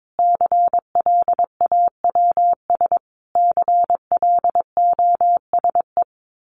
Morse code, heard as a keyed tone, about 22 wpm, 700 Hz. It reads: CLAWS CLOSE
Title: A Sound Of Thunder 22wpm